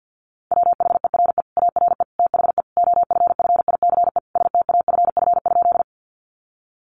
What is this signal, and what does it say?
Morse code, heard as a keyed tone, about 50 wpm, 700 Hz. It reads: WHERE ARE THE OFFICE STAFF?
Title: Sentences from Top 1200 Words 50wpm